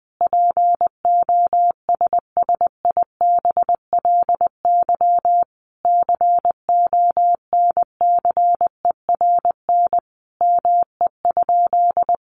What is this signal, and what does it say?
Morse code, heard as a keyed tone, about 20 wpm, 700 Hz. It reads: POSSIBLY CONCERN ME?
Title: Sentences from Top 1300 Words 20wpm